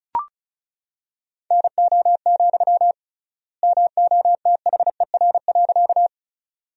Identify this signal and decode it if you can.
Morse code, heard as a keyed tone, about 36 wpm, 700 Hz. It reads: NO, MOTHER.
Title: 3x Word-Spacing - Sentences from Top 200 Words - 35wpm